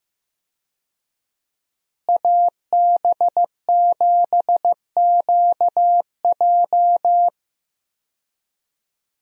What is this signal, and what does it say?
Morse code, heard as a keyed tone, about 15 wpm, 700 Hz. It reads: AB7QJ
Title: USA Call Signs - 5 Characters - Repeated 2x - 15wpm